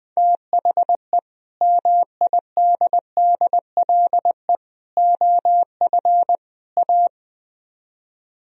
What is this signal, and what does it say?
Morse code, heard as a keyed tone, about 20 wpm, 700 Hz. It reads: THE MIDDLE OF A
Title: Speed-Racing - Sets of 4 Words 20wpm